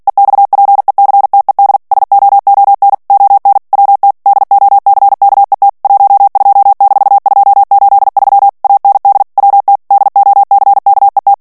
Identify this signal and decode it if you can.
Morse code, heard as a keyed tone, about 48 wpm, 800 Hz. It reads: EXPECTED SOON ON WT DOCKET 12-283 AND WT DOCKET
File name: ANGLAIS texte clair 3 vitesse 45 mots minute